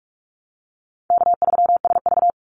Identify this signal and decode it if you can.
Morse code, heard as a keyed tone, about 46 wpm, 700 Hz. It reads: K3SV
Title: Easy Call Signs - 45wpm